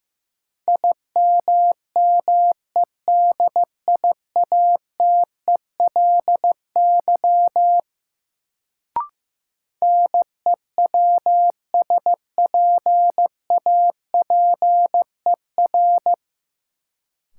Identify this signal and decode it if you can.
Morse code, heard as a keyed tone, about 15 wpm, 700 Hz. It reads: IMMEDIATELY NEWSPAPER
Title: Sentences from Top 1100 Words - New Word Drill Alt - 15wpm